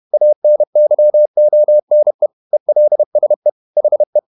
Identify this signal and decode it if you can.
Morse code, heard as a keyed tone, about 31 wpm, 600 Hz. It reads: ANYONE ELSE HE
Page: Never 31wpm | CW med Gnister